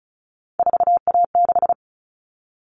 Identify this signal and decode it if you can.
Morse code, heard as a keyed tone, about 34 wpm, 700 Hz. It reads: 4A6
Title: All Cut Numbers - 3-Digit 35wpm